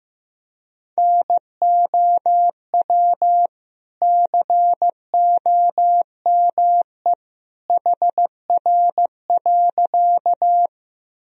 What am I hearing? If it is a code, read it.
Morse code, heard as a keyed tone, about 15 wpm, 700 Hz. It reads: NOW COME HR.
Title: Sentences from Top 200 Words and Abbreviations 15wpm